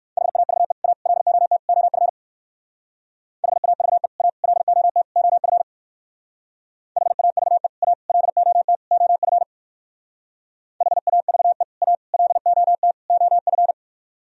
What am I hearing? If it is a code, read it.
Morse code, heard as a keyed tone, about 57 wpm, 700 Hz. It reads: HAVE A LOT OF HAVE A LOT OF HAVE A LOT OF HAVE A LOT OF